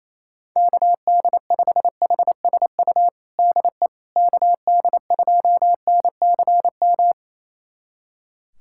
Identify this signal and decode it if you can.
Morse code, heard as a keyed tone, about 28 wpm, 700 Hz. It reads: KD5HSU DE KD2NCM
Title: USA Callsign DE Callsign - 6 Characters - 28wpm